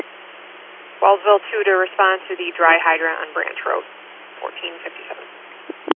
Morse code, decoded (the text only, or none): none